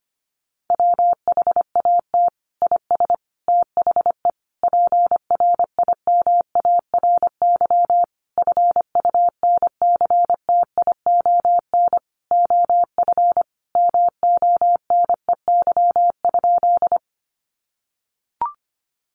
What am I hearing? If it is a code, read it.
Morse code, heard as a keyed tone, about 25 wpm, 690 Hz. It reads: WHAT IS THE PRIMARY FUNCTION OF MONEY?
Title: Sentences from Top 1100 Words 25wpm